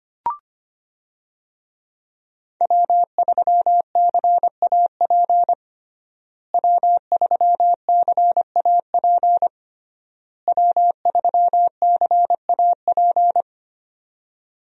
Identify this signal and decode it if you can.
Morse code, heard as a keyed tone, about 25 wpm, 700 Hz. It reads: W3CAP W3CAP W3CAP
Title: USA Call Signs - 5 Characters - Repeated 3x - 25wpm